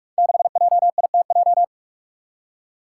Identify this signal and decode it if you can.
Morse code, heard as a keyed tone, about 46 wpm, 700 Hz. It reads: 6JITJ
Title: Five Challenging Letter-Number Combinations 45wpm